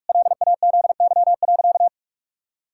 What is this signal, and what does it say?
Morse code, heard as a keyed tone, about 48 wpm, 700 Hz. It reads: LAZY.
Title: Pangram Challenge v2 - 45wpm